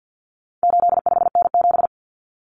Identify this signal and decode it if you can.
Morse code, heard as a keyed tone, about 50 wpm, 700 Hz. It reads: 85N7
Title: Common Cut Numbers - 4-Digit 50wpm